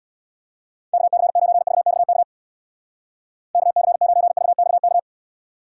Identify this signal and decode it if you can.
Morse code, heard as a keyed tone, about 61 wpm, 700 Hz. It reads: KC9VXK KC9VXK
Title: Speed Racing - USA Call Signs 50wpm